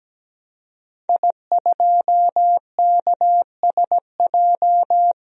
Morse code, heard as a keyed tone, about 17 wpm, 700 Hz. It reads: I2KSJ
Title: DXCC-100 Call Signs - 5 Characters - 17wpm